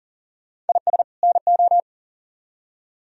Morse code, heard as a keyed tone, about 53 wpm, spaced as for 47 wpm, 700 Hz. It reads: IS NO
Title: Speed-Racing - Sets of 2 Words 40wpm